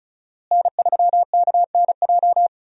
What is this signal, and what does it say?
Morse code, heard as a keyed tone, about 34 wpm, 700 Hz. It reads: N3KNJ
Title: USA Call Signs - 5 Characters - Repeated 2x - 35wpm